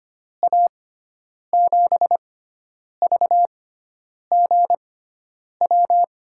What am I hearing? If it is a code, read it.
Morse code, heard as a keyed tone, about 25 wpm, spaced as for 9 wpm, 700 Hz. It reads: A7VGW
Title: USA Call Signs - 6 Characters - 6x Character Spacing - 25wpm